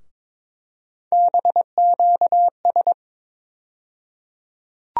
Morse code, heard as a keyed tone, about 22 wpm, 700 Hz. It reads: BQS
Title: USA Call Sign Postfixes - 3 Letters 22wpm